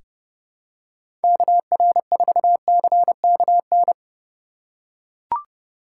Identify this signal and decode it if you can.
Morse code, heard as a keyed tone, about 30 wpm, 700 Hz. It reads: KR4CKN
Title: USA Call Signs - 6 Characters - 3x Character Spacing - 30wpm